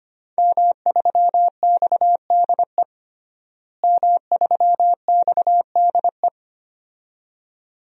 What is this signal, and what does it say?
Morse code, heard as a keyed tone, about 25 wpm, 700 Hz. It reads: M3XDE M3XDE
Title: DXCC-100 Call Signs - 5 Characters - Repeated 2x - 25wpm